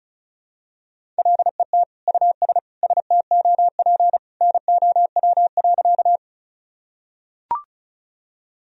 Morse code, heard as a keyed tone, about 36 wpm, 700 Hz. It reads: LET US STOP NOW.
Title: Sentences from Top 1000 Words - Review Part 1 - 35wpm